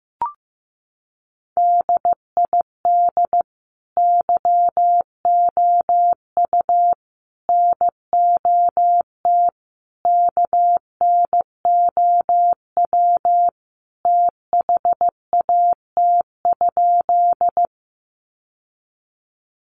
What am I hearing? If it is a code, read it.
Morse code, heard as a keyed tone, about 15 wpm, 700 Hz. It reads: DID YOU NOT KNOW THAT?
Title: Sentences from Top 600 Words 15wpm